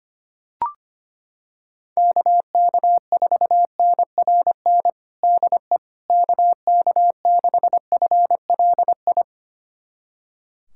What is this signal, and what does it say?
Morse code, heard as a keyed tone, about 25 wpm, 700 Hz. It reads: KK4NRN DE KK6FLI
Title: USA Callsign DE Callsign - 6 Characters - 25wpm